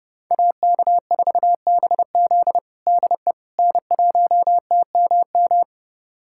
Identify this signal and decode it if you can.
Morse code, heard as a keyed tone, about 30 wpm, 700 Hz. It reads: AK4BZ DE N1TMM
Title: USA Callsign DE Callsign - 5 Characters - 30wpm